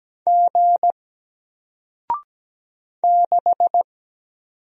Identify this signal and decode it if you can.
Morse code, heard as a keyed tone, about 17 wpm, 700 Hz. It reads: G 6
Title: Single Letter-Number 17wpm